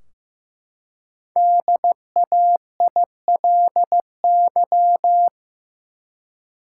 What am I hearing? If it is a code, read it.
Morse code, heard as a keyed tone, about 15 wpm, 700 Hz. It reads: DAILY
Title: Five Letter Words - 15wpm